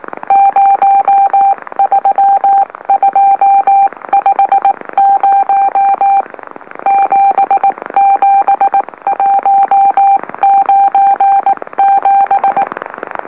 Morse code, heard as a keyed tone, about 18 wpm, 790 Hz. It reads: 03250 77197
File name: M04- -2345utc-LOLO